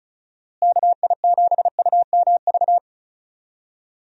Morse code, heard as a keyed tone, about 36 wpm, 700 Hz. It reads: KI7UMV
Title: USA Call Signs - 6 Characters - Repeated 3x - 35wpm